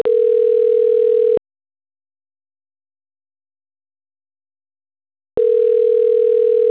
Ringback tone (North America), 440 + 480 Hz.
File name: autotestagent / 10-56-35.136..callcentertransfer. .0056e3b6-fd2a-4bea-8b6a